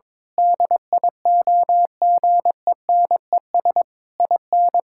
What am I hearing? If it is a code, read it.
Morse code, heard as a keyed tone, about 22 wpm, 700 Hz. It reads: DIOGENES IN
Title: Pangram Challenge v2 - 22wpm